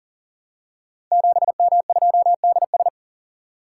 Morse code, heard as a keyed tone, about 39 wpm, 700 Hz. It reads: 7M2DS
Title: Code Groups - 5 Letter:Numbers - 40wpm